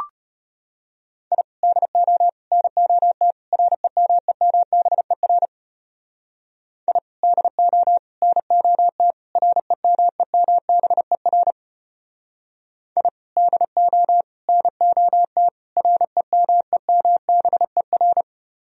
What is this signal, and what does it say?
Morse code, heard as a keyed tone, about 33 wpm, 700 Hz. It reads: I DO NOT REMEMBER I DO NOT REMEMBER I DO NOT REMEMBER